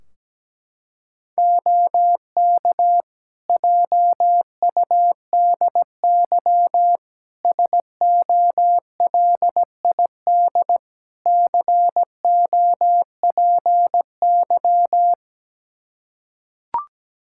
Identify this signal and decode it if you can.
Morse code, heard as a keyed tone, about 17 wpm, 700 Hz. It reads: OK JUDY SOLID COPY
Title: Practice QSO - Exchange 3 - More Info - 17wpm